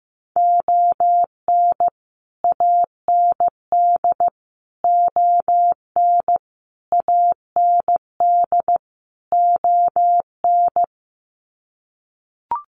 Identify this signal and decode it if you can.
Morse code, heard as a keyed tone, about 15 wpm, 700 Hz. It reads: ON AND ON AND ON
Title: Sets of 5 Words Encore2 15wpm